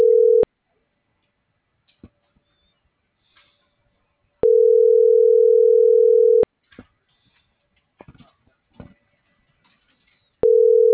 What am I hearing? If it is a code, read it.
Ringback tone (North America), 440 + 480 Hz.